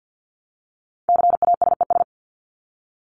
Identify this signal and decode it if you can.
Morse code, heard as a keyed tone, about 50 wpm, 700 Hz. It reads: CASES
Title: ES:S Suffix - Top 100 Words with Base Words 50wpm